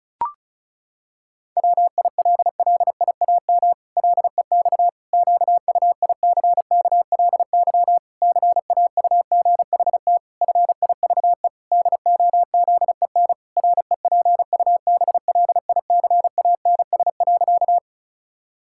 Morse code, heard as a keyed tone, about 36 wpm, 700 Hz. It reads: WILLIAM LEX QUICKLY CAUGHT FIVE DOZEN REPUBLICANS.
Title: Pangram Challenge v2 - 35wpm